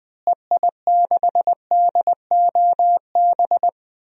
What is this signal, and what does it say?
Morse code, heard as a keyed tone, about 20 wpm, 700 Hz. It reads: EI6DOB
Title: DXCC-100 Call Signs - 6 Characters - Repeated 3x - 20wpm